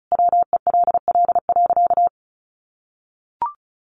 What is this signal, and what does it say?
Morse code, heard as a keyed tone, about 34 wpm, 700 Hz. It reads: WELL.
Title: Sentences from Top 800 Words 35wpm